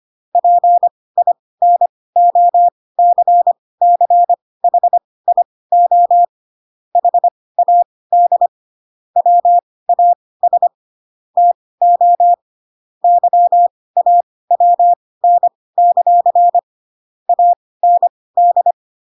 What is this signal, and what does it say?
Morse code, heard as a keyed tone, about 25 wpm, spaced as for 18 wpm, 700 Hz. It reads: PINOCCHIO HAD WAS TO YAWN; AND